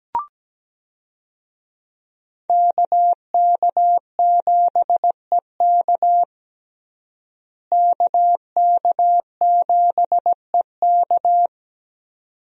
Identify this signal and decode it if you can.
Morse code, heard as a keyed tone, about 17 wpm, 700 Hz. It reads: KK7EK KK7EK
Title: USA Call Signs - 5 Characters - Repeated 2x - 17wpm